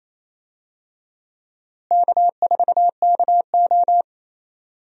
Morse code, heard as a keyed tone, about 28 wpm, 700 Hz. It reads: K4KO
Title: USA Call Signs - 4 Characters - Repeated 2x - 28wpm